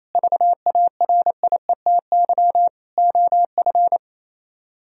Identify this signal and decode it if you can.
Morse code, heard as a keyed tone, about 34 wpm, 700 Hz. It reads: VARIETY OF
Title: Speed-Racing - Sets of 4 Words 28wpm